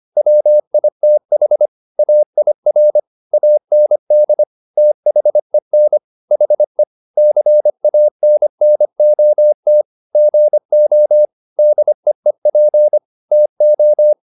Morse code, heard as a keyed tone, about 25 wpm, 600 Hz. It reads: WITH AIR AND THEN HE CANNOT GO DEEP TO
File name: the old man and the sea part 1 25wpm